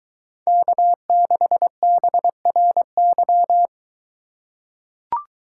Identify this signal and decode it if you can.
Morse code, heard as a keyed tone, about 23 wpm, 700 Hz. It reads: K6BRY